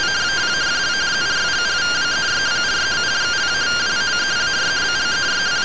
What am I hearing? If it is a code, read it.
Two-tone teleprinter signal, about 170 Hz shift.
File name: findings / MR AXIAL 0mm 6mm.gen-21 rtty rx